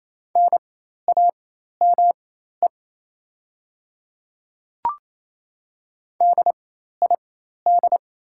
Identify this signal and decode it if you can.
Morse code, heard as a keyed tone, about 28 wpm, spaced as for 13 wpm, 690 Hz. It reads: NAME DID
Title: 4x Character Spacing - Top 200 Words - 28wpm